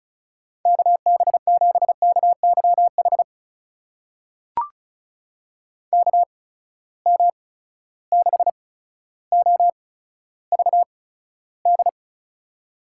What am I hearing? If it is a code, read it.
Morse code, heard as a keyed tone, about 36 wpm, spaced as for 15 wpm, 700 Hz. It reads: KB7KYH KM6OVD
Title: USA Call Signs - 6 Characters - 8x Character Spacing - 35wpm